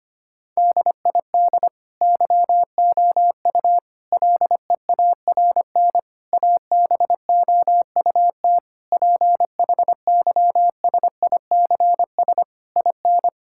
Morse code, heard as a keyed tone, about 25 wpm, 700 Hz. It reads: DID YOU LEARN ABOUT PHYSICS IN
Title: Sentences from Top 1000 Words - Review Part 5 - 25wpm